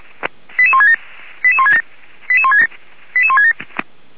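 Five-tone selective call (ZVEI), heard as quick runs of five tones, tones of about 70 ms. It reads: 88167 88167
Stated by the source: ZVEI five-tone selective call 88153, received via SDR.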